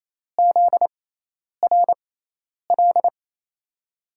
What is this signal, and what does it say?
Morse code, heard as a keyed tone, about 28 wpm, spaced as for 10 wpm, 700 Hz. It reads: ZRL
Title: USA Call Signs - 5 Characters - 6x Character Spacing - 28wpm